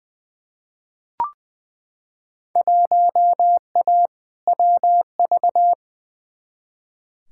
Morse code, heard as a keyed tone, about 20 wpm, 700 Hz. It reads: 1A WV
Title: Field Day Exchanges Encore - 20wpm